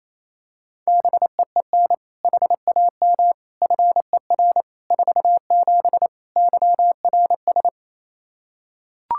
Morse code, heard as a keyed tone, about 28 wpm, 700 Hz. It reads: BEEN HAM FER 47 YRS
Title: Practice QSO - Exchange 3 - More Info - 28wpm